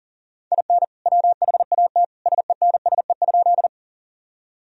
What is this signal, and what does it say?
Morse code, heard as a keyed tone, about 37 wpm, 700 Hz. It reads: IN WHAT SENSE?
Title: Sentences from Top 1000 Words - Review Part 4 - 40wpm